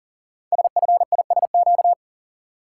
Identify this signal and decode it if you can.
Morse code, heard as a keyed tone, about 40 wpm, 700 Hz. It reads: SFISQ
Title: Code Groups - 5 Letter:Numbers - 40wpm